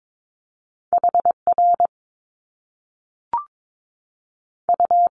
Morse code, heard as a keyed tone, about 22 wpm, spaced as for 3 wpm, 700 Hz. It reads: H RU
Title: Top Two Letter Combinations in Words 22wpm